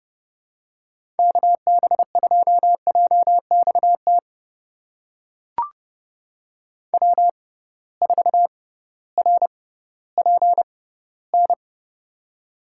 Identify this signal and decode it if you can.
Morse code, heard as a keyed tone, about 32 wpm, spaced as for 17 wpm, 700 Hz. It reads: KB2JXT W4RPN
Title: USA Call Signs - 6x Character Spacing - 30wpm